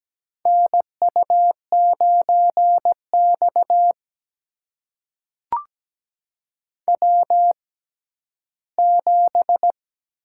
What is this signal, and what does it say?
Morse code, heard as a keyed tone, about 17 wpm, spaced as for 12 wpm, 700 Hz. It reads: NU9X W7
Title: USA Call Signs - 4 Characters - 6x Character Spacing - 17wpm